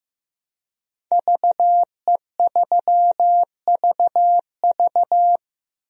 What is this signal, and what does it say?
Morse code, heard as a keyed tone, about 15 wpm, 700 Hz. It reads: VE3VV
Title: DXCC-100 Call Signs - 5 Characters - Repeated 3x - 15wpm